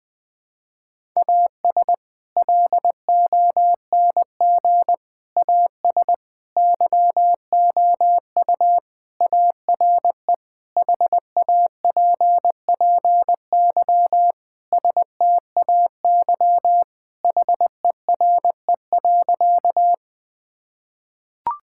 Morse code, heard as a keyed tone, about 20 wpm, 700 Hz. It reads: AS LONG AS YOU ARE HAPPY STAY HERE.
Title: Sentences with 4-Word Phrases - Spoken Beginning - 20wpm